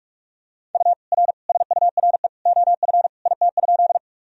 Morse code, heard as a keyed tone, about 46 wpm, 700 Hz. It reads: U R SURE OF IT?
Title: Sentences from Top 300 Words and Abbreviations 45wpm